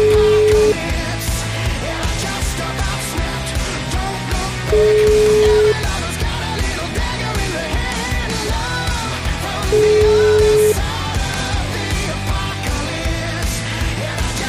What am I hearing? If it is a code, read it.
Ringback tone (Europe, 425 Hz), with noise.